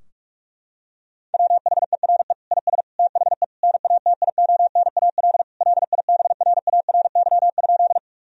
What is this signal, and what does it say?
Morse code, heard as a keyed tone, about 41 wpm, 700 Hz. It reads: WHERE IS THE NATIONAL LIBRARY?
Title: Sentences from Top 1000 Words - Review Part 4 - 45wpm